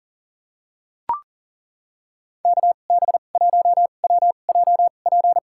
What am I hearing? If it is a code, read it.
Morse code, heard as a keyed tone, about 40 wpm, spaced as for 29 wpm, 700 Hz. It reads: KB1WJP
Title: USA Call Signs - 6 Characters - 2x Character Spacing - 40wpm